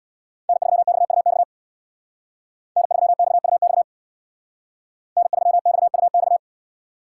Morse code, heard as a keyed tone, about 59 wpm, 700 Hz. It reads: N3XUX N3XUX N3XUX
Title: Speed Racing - USA 1x3 Call Signs 50wpm